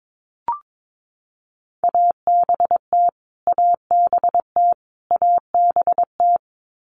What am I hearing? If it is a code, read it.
Morse code, heard as a keyed tone, about 22 wpm, 700 Hz. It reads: ABT ABT ABT
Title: Abbreviations Repeated 3x - 22wpm